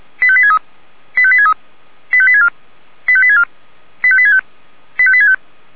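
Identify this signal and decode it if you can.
Five-tone selective call (ZVEI), heard as quick runs of five tones, tones of about 70 ms. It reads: 85863 85863 85864 85864 85865 85865
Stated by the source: SDR-received ZVEI five-tone selective call, code 85861.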